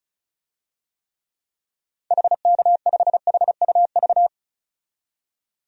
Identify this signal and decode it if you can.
Morse code, heard as a keyed tone, about 34 wpm, 700 Hz. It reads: HK5HUV
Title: DXCC-100 Call Signs - 6 Characters - 35wpm